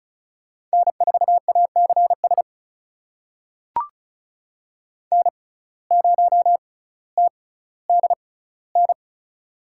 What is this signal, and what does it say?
Morse code, heard as a keyed tone, about 36 wpm, spaced as for 18 wpm, 700 Hz. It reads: N4ACS N0TDN
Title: USA Call Signs - 6x Character Spacing - 35wpm